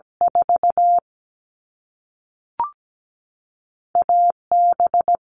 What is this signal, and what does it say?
Morse code, heard as a keyed tone, about 17 wpm, 700 Hz. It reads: 4 AB
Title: All Cut Numbers - 2-Digit 17wpm